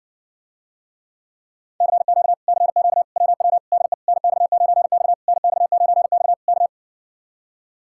Morse code, heard as a keyed tone, about 69 wpm, 700 Hz. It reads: CQ CQ WW DE NX9X NX9X K